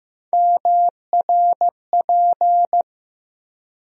Morse code, heard as a keyed tone, about 15 wpm, 700 Hz. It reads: MRP
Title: USA Call Sign Postfixes - 3 Letters 15wpm